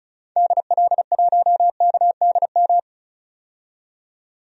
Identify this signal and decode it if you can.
Morse code, heard as a keyed tone, about 36 wpm, 700 Hz. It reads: DL1KDM
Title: DXCC-100 Call Signs - 6 Characters - 35wpm